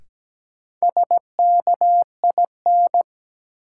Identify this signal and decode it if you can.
Morse code, heard as a keyed tone, about 17 wpm, 700 Hz. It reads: SKIN
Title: Sentences from Top 1300 Words - New Word Drill - 17wpm